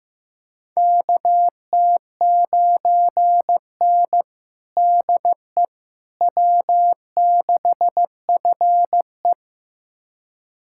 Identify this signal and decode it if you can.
Morse code, heard as a keyed tone, about 15 wpm, 700 Hz. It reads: KT9N DE W6FE
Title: USA Callsign DE Callsign - 4 Characters - 15wpm